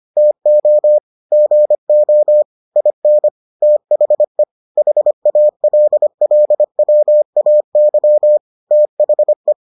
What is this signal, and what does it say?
Morse code, heard as a keyed tone, about 25 wpm, 600 Hz. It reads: TO GO IN THE HALLWAY THE